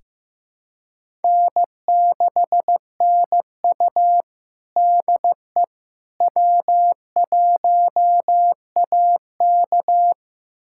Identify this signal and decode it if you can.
Morse code, heard as a keyed tone, about 15 wpm, 700 Hz. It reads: N6NU DE W1AK
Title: USA Callsign DE Callsign - 4 Characters - 15wpm